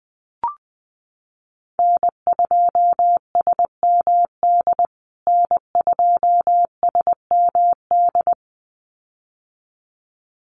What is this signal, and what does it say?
Morse code, heard as a keyed tone, about 20 wpm, 700 Hz. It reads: N2SMD N2SMD
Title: USA Call Signs Repeated 2x 20wpm